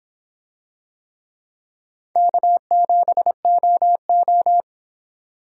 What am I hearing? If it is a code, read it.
Morse code, heard as a keyed tone, about 26 wpm, 700 Hz. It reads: K7OO